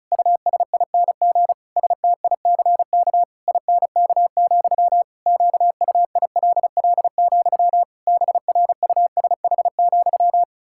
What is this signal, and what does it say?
Morse code, heard as a keyed tone, about 36 wpm, 700 Hz. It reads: USING STICK INK, QUILL, BRUSH,
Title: Pangram Challenge v2 - 35wpm